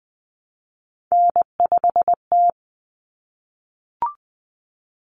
Morse code, heard as a keyed tone, about 20 wpm, 700 Hz. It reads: N5T
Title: Common Cut Numbers - 3-Digit 20wpm